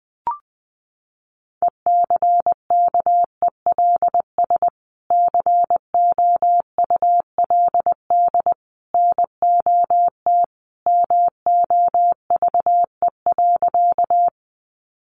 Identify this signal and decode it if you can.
Morse code, heard as a keyed tone, about 20 wpm, 700 Hz. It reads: ECKELS COULD NOT MOVE.
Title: A Sound Of Thunder 20wpm